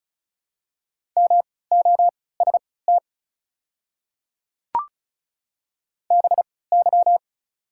Morse code, heard as a keyed tone, about 36 wpm, spaced as for 20 wpm, 700 Hz. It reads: MOST BY
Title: 3x Character Spacing - Top 200 Words - 35wpm